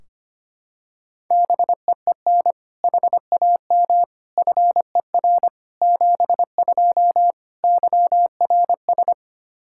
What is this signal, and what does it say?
Morse code, heard as a keyed tone, about 25 wpm, 700 Hz. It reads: BEEN HAM FER 72 YRS
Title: K1USN SST QSO Practice 25wpm